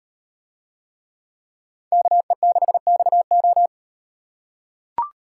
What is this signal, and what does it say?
Morse code, heard as a keyed tone, about 36 wpm, 700 Hz. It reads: KE6XO